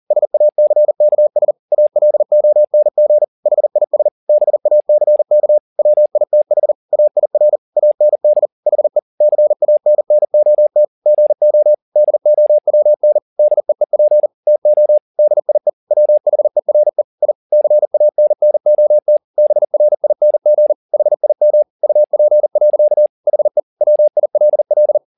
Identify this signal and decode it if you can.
Morse code, heard as a keyed tone, about 40 wpm, 600 Hz. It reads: SACKS ALONG HIS BACK WITH AIR AND HE CANNOT GO DOWN DEEP TO DIE WHERE I CANNOT BRING HIM UP. HE WILL
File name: the old man and the sea 2 40wpm